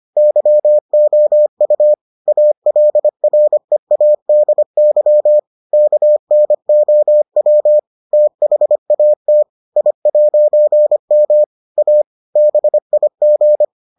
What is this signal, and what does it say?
Morse code, heard as a keyed tone, about 25 wpm, 600 Hz. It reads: YOU ALREADY KNOW THAT I'M A BIG